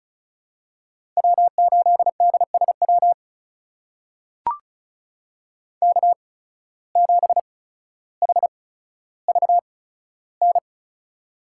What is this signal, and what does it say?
Morse code, heard as a keyed tone, about 34 wpm, spaced as for 15 wpm, 700 Hz. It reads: W8DSW K7HVN
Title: USA Call Signs - 5 Characters - 8x Character Spacing - 35wpm